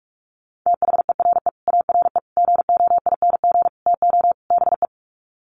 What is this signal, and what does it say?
Morse code, heard as a keyed tone, about 44 wpm, 700 Hz. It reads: THERE ARE GOING TO BE
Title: Sets of 5 Words 45wpm